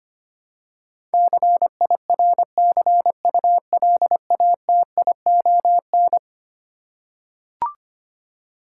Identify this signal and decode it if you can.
Morse code, heard as a keyed tone, about 25 wpm, 700 Hz. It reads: CIRCULATION
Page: morse-code-ninja-practice-25wpm